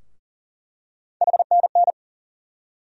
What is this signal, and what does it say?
Morse code, heard as a keyed tone, about 40 wpm, 700 Hz. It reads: HNN
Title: USA Call Sign Postfixes - 3 Letters 40wpm